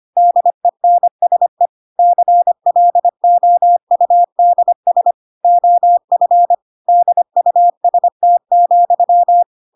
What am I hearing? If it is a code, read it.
Morse code, heard as a keyed tone, about 25 wpm, 700 Hz. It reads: DENSE CLOUDS OF DUST,
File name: Pinocchio / Pinocchio ch6 25 25 ik1hge